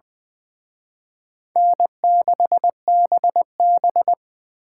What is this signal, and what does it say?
Morse code, heard as a keyed tone, about 24 wpm, 700 Hz. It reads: N6BB
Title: Speed Racing - USA 1x2 Call Signs 20wpm